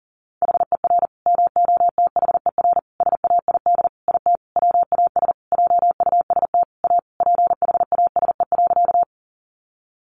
Morse code, heard as a keyed tone, about 40 wpm, 700 Hz. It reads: HER MOTHER SAID IT WAS JUST A PHASE.
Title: Sentences from Top 1200 Words 40wpm